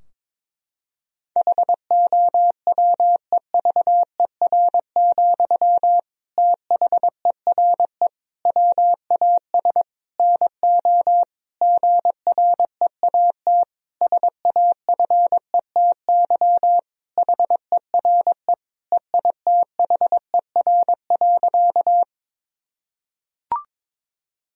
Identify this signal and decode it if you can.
Morse code, heard as a keyed tone, about 22 wpm, 700 Hz. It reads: HOWEVER, THERE WAS NO GREAT SAFETY HERE EITHER.
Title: Sentences from Top 1000 Words - Review Part 4 - 22wpm